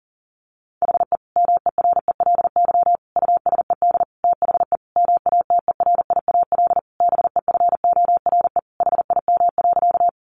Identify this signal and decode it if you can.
Morse code, heard as a keyed tone, about 40 wpm, 700 Hz. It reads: HE MERELY USED THE MATERIAL BEFORE HIM.
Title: Sentences from Top 1400 Words 40wpm